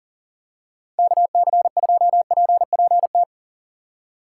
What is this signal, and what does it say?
Morse code, heard as a keyed tone, about 38 wpm, 700 Hz. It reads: KC2PPT
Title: USA Call Signs - 6 Characters - 3x Character Spacing - 40wpm